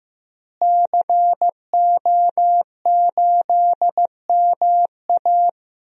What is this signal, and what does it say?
Morse code, heard as a keyed tone, about 15 wpm, 700 Hz. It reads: CO8MA
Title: DXCC-100 Call Signs - 5 Characters - Repeated 2x - 15wpm